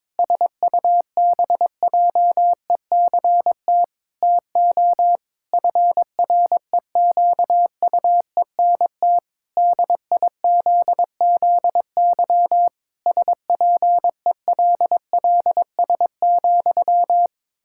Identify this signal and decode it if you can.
Morse code, heard as a keyed tone, about 22 wpm, 700 Hz. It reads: SUBJECT TO FREQUENT DIZZY SPELLS,
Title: Pangram Challenge v2 - 22wpm